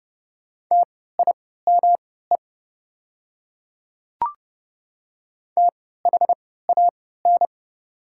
Morse code, heard as a keyed tone, about 30 wpm, spaced as for 17 wpm, 700 Hz. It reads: TIME THAN
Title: 3x Character Spacing - Top 300 Words - 30wpm